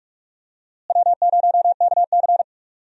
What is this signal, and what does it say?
Morse code, heard as a keyed tone, about 46 wpm, 700 Hz. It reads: W0KC
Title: USA Call Signs - 4 Characters - Repeated 3x - 45wpm